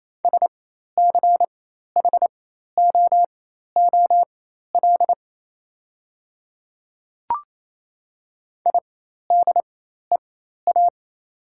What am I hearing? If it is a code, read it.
Morse code, heard as a keyed tone, about 28 wpm, spaced as for 13 wpm, 700 Hz. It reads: SCHOOL IDEA
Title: 4x Character Spacing - Top 300 Words - 28wpm